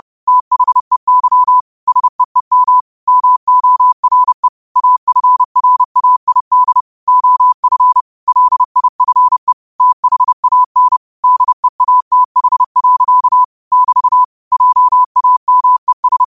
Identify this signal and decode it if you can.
Morse code, heard as a keyed tone, about 30 wpm, 1000 Hz. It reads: THEY SEEM MORE AFRAID OF LIFE THAN DEATH. <BT> JAMES
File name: qotdfiles / 30wpm